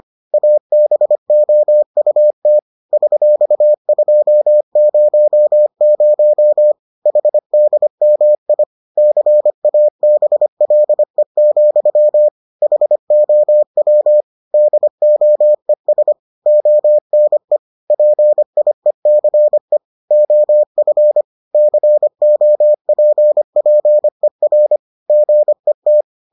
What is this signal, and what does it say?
Morse code, heard as a keyed tone, about 25 wpm, 600 Hz. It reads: ABOUT $200 HDMI CABLE, HOW DOES ONE PIECE OF COPPER GET